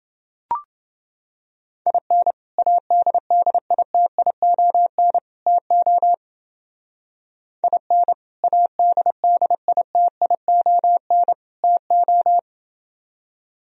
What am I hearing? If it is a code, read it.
Morse code, heard as a keyed tone, about 28 wpm, 700 Hz. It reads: IN ADDITION TO IN ADDITION TO